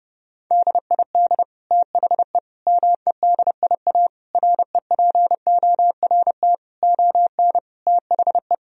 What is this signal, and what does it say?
Morse code, heard as a keyed tone, about 30 wpm, 700 Hz. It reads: DID THE MEDIA REPORT ON THE
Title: Sentences from Top 1000 Words - Review Part 4 - 30wpm